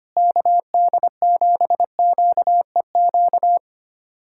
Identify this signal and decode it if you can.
Morse code, heard as a keyed tone, about 25 wpm, 700 Hz. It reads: KD7QEQ
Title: USA Callsign DE Callsign 25wpm